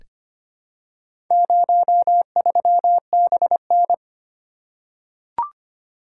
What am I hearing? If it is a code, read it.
Morse code, heard as a keyed tone, about 25 wpm, 700 Hz. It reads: 03BN
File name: practice / 25wpm / Four Letter-Number Combinations - 25wpm